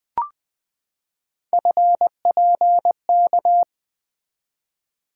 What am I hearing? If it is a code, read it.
Morse code, heard as a keyed tone, about 20 wpm, 700 Hz. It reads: FPK
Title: Three Letter-Number Combinations - 20wpm